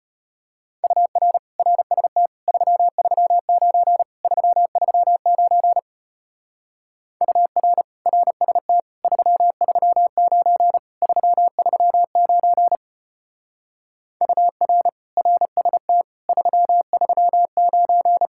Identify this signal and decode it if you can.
Morse code, heard as a keyed tone, about 30 wpm, 700 Hz. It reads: UR RST 339 339 UR RST 339 339 UR RST 339
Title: Speed-Racing Common QSO Phrases - 25wpm